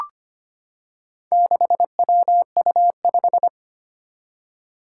Morse code, heard as a keyed tone, about 25 wpm, 700 Hz. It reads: 6WU5
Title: Four Challenging Letter-Number Combinations 25wpm